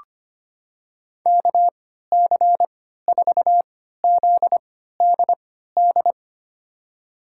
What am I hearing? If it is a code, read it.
Morse code, heard as a keyed tone, about 25 wpm, spaced as for 14 wpm, 700 Hz. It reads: KC4ZDD
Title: USA Call Signs - 6 Characters - 3x Character Spacing - 25wpm